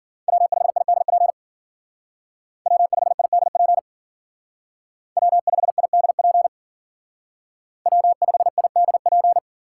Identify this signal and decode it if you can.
Morse code, heard as a keyed tone, about 50 wpm, 700 Hz. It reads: W5IDP W5IDP W5IDP W5IDP
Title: Speed Racing - USA Call Signs 40wpm